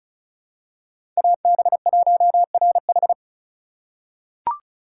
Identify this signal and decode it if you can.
Morse code, heard as a keyed tone, about 36 wpm, 700 Hz. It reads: AB1RH
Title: USA Call Signs - 5 Characters - 4x Character Spacing - 35wpm